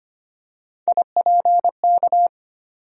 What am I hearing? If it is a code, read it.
Morse code, heard as a keyed tone, about 25 wpm, 700 Hz. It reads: IPK
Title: USA Call Sign Postfixes - 3 Letters 25wpm